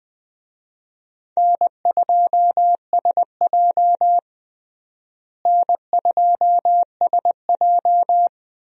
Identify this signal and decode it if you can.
Morse code, heard as a keyed tone, about 20 wpm, 700 Hz. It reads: N2SJ N2SJ
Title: USA Call Signs - 4 Characters - Repeated 2x - 20wpm